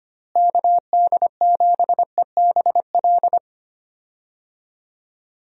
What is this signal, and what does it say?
Morse code, heard as a keyed tone, about 25 wpm, 700 Hz. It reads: KD7EBL
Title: USA Call Signs - 6 Characters - Repeated 3x - 25wpm